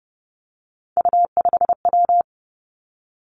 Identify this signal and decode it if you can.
Morse code, heard as a keyed tone, about 30 wpm, 700 Hz. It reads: U5W
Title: Three Challenging Letter-Number Combinations 30wpm